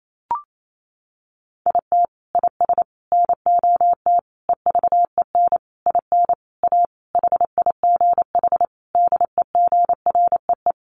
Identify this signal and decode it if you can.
Morse code, heard as a keyed tone, about 28 wpm, 700 Hz. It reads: IT IS NOT EVEN IN A HIGH DEGREE
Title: Sentences from Top 1500 Words 28wpm